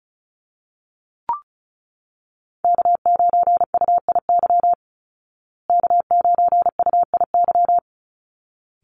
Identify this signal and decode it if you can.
Morse code, heard as a keyed tone, about 34 wpm, 700 Hz. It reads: K9UIY K9UIY
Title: Call Signs Repeated 3x - 35wpm